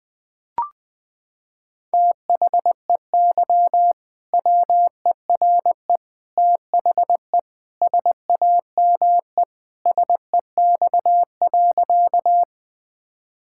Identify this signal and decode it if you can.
Morse code, heard as a keyed tone, about 20 wpm, 700 Hz. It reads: THEY WERE THE SAME SEX.
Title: Sentences from Top 1600 Words 20wpm